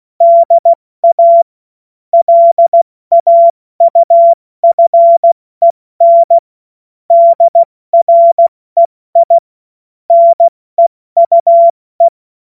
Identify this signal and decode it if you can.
Morse code, heard as a keyed tone, about 16 wpm, 680 Hz. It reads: DA LAUFEN DREI NEUE
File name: Morsekurs / Witze / Witz 27. April 2021, 70 F80